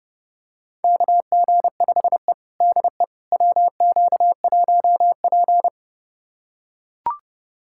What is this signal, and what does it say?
Morse code, heard as a keyed tone, about 32 wpm, 700 Hz. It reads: KG5E DE WQ1P
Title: USA Callsign DE Callsign - 4 Characters - 30wpm